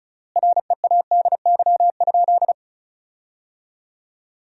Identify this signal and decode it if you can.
Morse code, heard as a keyed tone, about 36 wpm, 700 Hz. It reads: READY?
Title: 3x Word-Spacing - Sentences from Top 400 Words - 35wpm